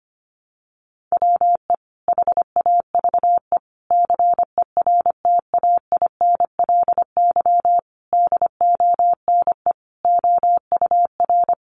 Morse code, heard as a keyed tone, about 25 wpm, 700 Hz. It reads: WE HAVE CERTAINLY DONE OUR
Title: Sentences from Top 900 Words 25wpm